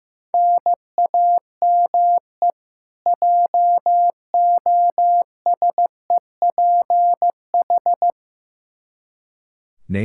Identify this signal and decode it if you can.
Morse code, heard as a keyed tone, about 15 wpm, 700 Hz. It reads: NAME JOSEPH
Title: Practice QSO - Exchange 1 - Beginning - 15wpm